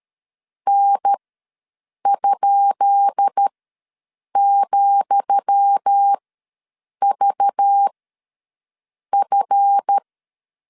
Morse code, heard as a keyed tone, about 13 wpm, spaced as for 7 wpm, 800 Hz. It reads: N?,VF
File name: LEZIONI / 10 SIGNS / 10D COMA ws 12wpm 3618